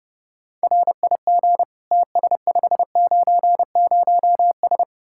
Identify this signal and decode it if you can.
Morse code, heard as a keyed tone, about 32 wpm, 700 Hz. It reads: RIG TS590S